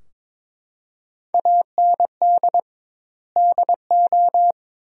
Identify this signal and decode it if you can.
Morse code, heard as a keyed tone, about 22 wpm, 700 Hz. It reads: AND DO
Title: 2x Word-Spacing - Sentences from Top 100 Words - 22wpm